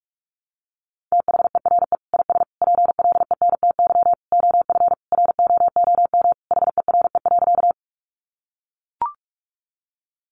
Morse code, heard as a keyed tone, about 44 wpm, 700 Hz. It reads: THERE IS PLENTY OF ROOM HERE.
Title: Sentences from Top 1300 Words 45wpm